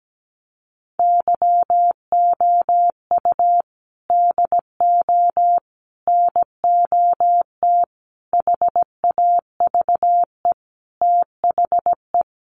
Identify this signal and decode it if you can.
Morse code, heard as a keyed tone, about 17 wpm, 700 Hz. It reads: YOU DO NOT HAVE THE
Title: Sets of 5 Words 17wpm